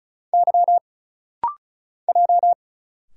Morse code, heard as a keyed tone, about 36 wpm, 700 Hz. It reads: Y J
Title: Single Letters - Rapid-Fire 35wpm